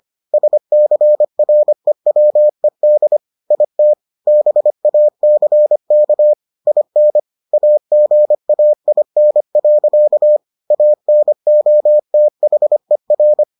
Morse code, heard as a keyed tone, about 25 wpm, 600 Hz. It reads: SCREWED IT BACK IN AGAIN. ANOTHER